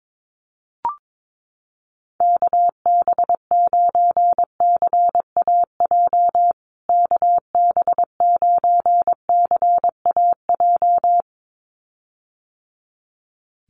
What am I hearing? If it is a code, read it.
Morse code, heard as a keyed tone, about 22 wpm, 700 Hz. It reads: KB9CAJ KB9CAJ
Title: USA Call Signs Repeated 2x 22wpm